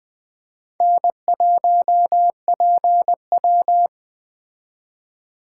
Morse code, heard as a keyed tone, about 20 wpm, 700 Hz. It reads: N1PW